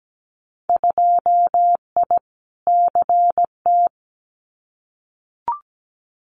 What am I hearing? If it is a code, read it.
Morse code, heard as a keyed tone, about 17 wpm, 700 Hz. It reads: 2I CT
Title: Winter Field Day Exchanges 17wpm